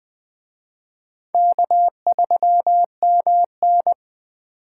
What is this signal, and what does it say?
Morse code, heard as a keyed tone, about 24 wpm, 700 Hz. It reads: K3MN
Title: Speed Racing - USA 1x2 Call Signs 20wpm